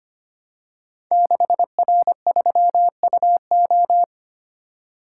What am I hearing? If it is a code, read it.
Morse code, heard as a keyed tone, about 25 wpm, 700 Hz. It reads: 6R3UO
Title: Code Groups - 5 Letter:Numbers - 25wpm